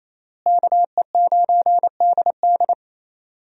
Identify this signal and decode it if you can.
Morse code, heard as a keyed tone, about 28 wpm, 700 Hz. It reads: KE9DD
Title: USA Call Signs - 5 Characters - Repeated 3x - 28wpm